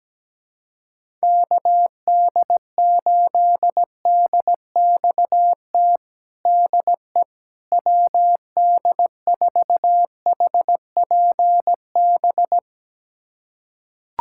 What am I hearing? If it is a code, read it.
Morse code, heard as a keyed tone, about 17 wpm, 700 Hz. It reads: KD8DXT DE WD4HPB
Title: USA Callsign DE Callsign - 6 Characters - 17wpm